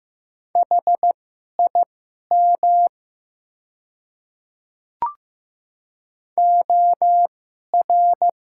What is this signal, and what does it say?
Morse code, heard as a keyed tone, about 15 wpm, spaced as for 11 wpm, 700 Hz. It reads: HIM OR
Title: 2x Character Spacing - Top 100 Words - 15wpm